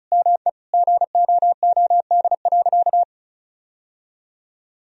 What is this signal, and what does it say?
Morse code, heard as a keyed tone, about 36 wpm, 700 Hz. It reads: ME GOOD.
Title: Sentences from Top 1000 Words - Review Part 1 - 35wpm